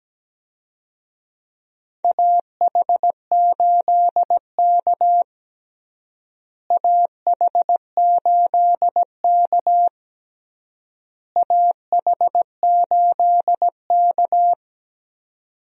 Morse code, heard as a keyed tone, about 17 wpm, 700 Hz. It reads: AH8K AH8K AH8K
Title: USA Call Signs - 4 Characters - Repeated 3x - 17wpm